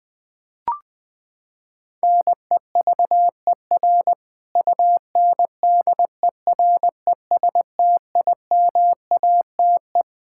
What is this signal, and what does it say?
Morse code, heard as a keyed tone, about 20 wpm, 700 Hz. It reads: NEVER UNDERESTIMATE
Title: Sentences with 4-Word Phrases - 20wpm